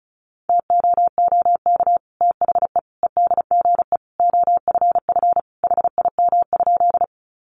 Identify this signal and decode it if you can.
Morse code, heard as a keyed tone, about 34 wpm, 700 Hz. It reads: TOOK THE EDGE OFF HIM?
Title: Sentences from Top 1300 Words 35wpm